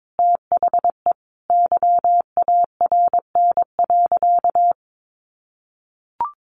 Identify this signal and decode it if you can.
Morse code, heard as a keyed tone, about 22 wpm, 700 Hz. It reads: THE YARN.
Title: Additional Easy Phrases 22wpm